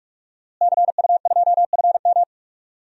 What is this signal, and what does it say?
Morse code, heard as a keyed tone, about 41 wpm, 700 Hz. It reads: CU2FM
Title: DXCC-100 Call Signs - 5 Characters - Repeated 3x - 45wpm